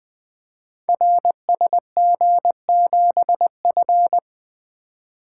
Morse code, heard as a keyed tone, about 20 wpm, 700 Hz. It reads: RSG7F
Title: Code Groups - 5 Letter:Numbers - 20wpm